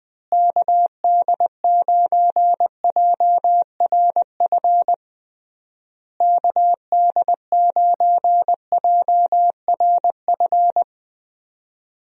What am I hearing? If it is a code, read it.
Morse code, heard as a keyed tone, about 20 wpm, 700 Hz. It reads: KD9JRF KD9JRF
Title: USA Call Signs - 6 Characters - Repeated 3x - 20wpm